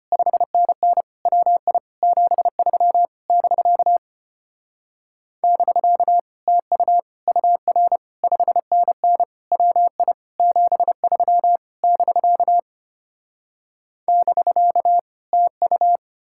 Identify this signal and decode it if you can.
Morse code, heard as a keyed tone, about 33 wpm, 700 Hz. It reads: 5NN WI 73 <BK> <BK> TU UR 5NN WI 73 <BK> <BK> TU